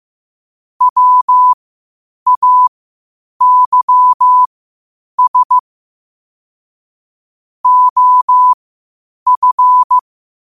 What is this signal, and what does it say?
Morse code, heard as a keyed tone, about 15 wpm, spaced as for 8 wpm, 1000 Hz. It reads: WAYS OF
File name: qotdfiles / 8wpm